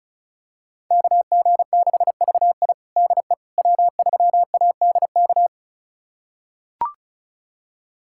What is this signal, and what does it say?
Morse code, heard as a keyed tone, about 36 wpm, 700 Hz. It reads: KG6VI DE W3ADK
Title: USA Callsign DE Callsign - 5 Characters - 35wpm